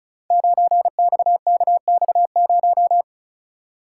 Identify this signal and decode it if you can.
Morse code, heard as a keyed tone, about 36 wpm, 700 Hz. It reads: 9XKX0
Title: Code Groups - 5 Letter:Numbers - 35wpm